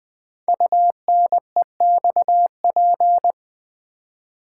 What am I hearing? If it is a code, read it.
Morse code, heard as a keyed tone, about 20 wpm, 700 Hz. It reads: UNEXP
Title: Build A Word - 5 Letters At A Time - 10-Letter Words - 20wpm